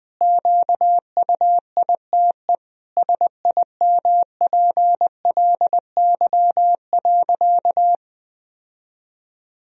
Morse code, heard as a keyed tone, about 21 wpm, 700 Hz. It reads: QUITE SIMPLY.
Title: Sentences from Top 1000 Words - Review Part 4 - 20wpm